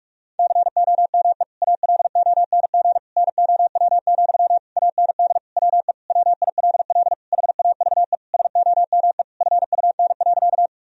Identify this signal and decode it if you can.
Morse code, heard as a keyed tone, about 41 wpm, 700 Hz. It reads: COME ALONG NOW, AND WE WILL HAVE SOME FUN.
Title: Sentences from Top 1000 Words - Review Part 4 - 45wpm